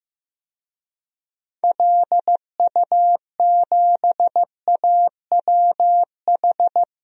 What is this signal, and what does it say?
Morse code, heard as a keyed tone, about 15 wpm, 700 Hz. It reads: LU7AWH
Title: DXCC-100 Call Signs - 6 Characters - 15wpm